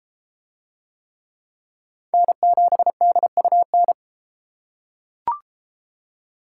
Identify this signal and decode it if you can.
Morse code, heard as a keyed tone, about 30 wpm, 700 Hz. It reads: N7DUN